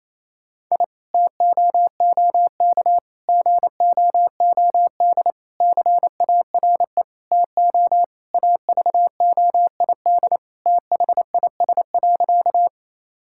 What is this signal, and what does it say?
Morse code, heard as a keyed tone, about 28 wpm, 700 Hz. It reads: I TOOK GOOD CARE TO AVOID THIS.
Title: Sentences from Top 1000 Words - Review Part 6 - 28wpm